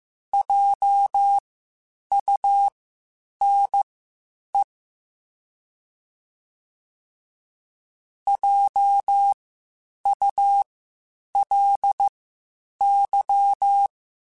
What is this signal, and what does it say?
Morse code, heard as a keyed tone, about 15 wpm, spaced as for 8 wpm, 800 Hz. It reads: JUNE JULY